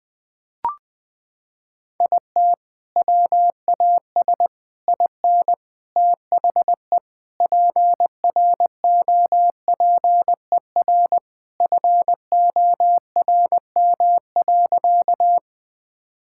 Morse code, heard as a keyed tone, about 20 wpm, 700 Hz. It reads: IT WAS IN THE PROPER FORM.
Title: Sentences from Top 1000 Words - Review Part 5 - 20wpm